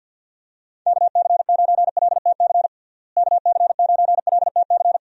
Morse code, heard as a keyed tone, about 50 wpm, 700 Hz. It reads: KC9LTC KC9LTC
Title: USA Call Signs - 6 Characters - Repeated 2x - 50wpm